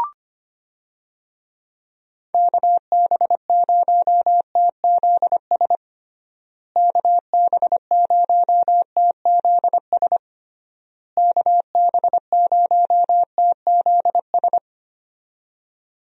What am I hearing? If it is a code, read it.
Morse code, heard as a keyed tone, about 25 wpm, 700 Hz. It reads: KB0TZS KB0TZS KB0TZS
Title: USA Call Signs - 6 Characters - Repeated 3x - 25wpm